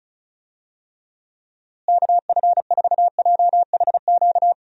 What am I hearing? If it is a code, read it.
Morse code, heard as a keyed tone, about 34 wpm, 700 Hz. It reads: KF4JHQ
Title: USA Call Signs - 6 Characters - Repeated 3x - 35wpm